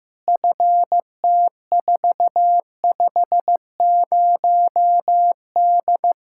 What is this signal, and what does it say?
Morse code, heard as a keyed tone, about 15 wpm, 700 Hz. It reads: FT450D
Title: Practice QSO - Exchange 2 - Info - 15wpm